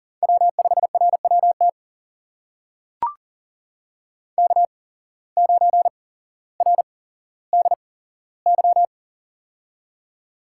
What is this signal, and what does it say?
Morse code, heard as a keyed tone, about 39 wpm, spaced as for 17 wpm, 700 Hz. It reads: W5RWT K9RDY
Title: USA Call Signs - 5 Characters - 8x Character Spacing - 40wpm